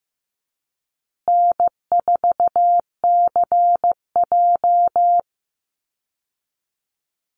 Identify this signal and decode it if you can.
Morse code, heard as a keyed tone, about 15 wpm, 700 Hz. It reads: N4CJ
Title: Easy Call Signs Repeated 3x - 15wpm